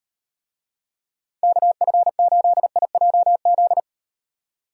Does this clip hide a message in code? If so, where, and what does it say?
Yes: Morse code from 1.4 to 3.8 s, heard as a keyed tone, about 38 wpm, 700 Hz. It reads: KF8IJZ